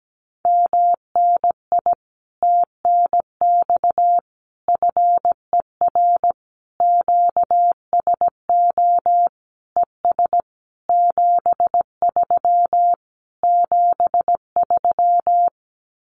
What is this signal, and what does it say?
Morse code, heard as a keyed tone, about 17 wpm, 700 Hz. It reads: MNI TNX FER QSO ES 73 73
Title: Common QSO Phrases Encore 17wpm